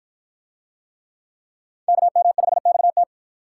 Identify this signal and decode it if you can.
Morse code, heard as a keyed tone, about 49 wpm, 700 Hz. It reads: KM5XT